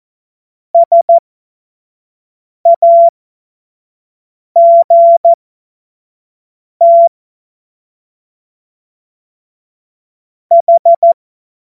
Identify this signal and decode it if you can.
Morse code, heard as a keyed tone, about 14 wpm, spaced as for 5 wpm, 680 Hz. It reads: SAGT H
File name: Morsekurs / Witze / Witz 06. Januar 2026 25 F70